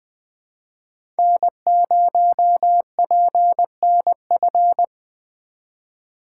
Morse code, heard as a keyed tone, about 20 wpm, 700 Hz. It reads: N0PNF
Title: USA Call Signs - 5 Characters - 2x Character Spacing - 20wpm